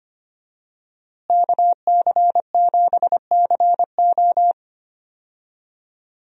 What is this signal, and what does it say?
Morse code, heard as a keyed tone, about 25 wpm, 700 Hz. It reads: KC7CO
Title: USA Call Signs - 5 Characters - Repeated 2x - 25wpm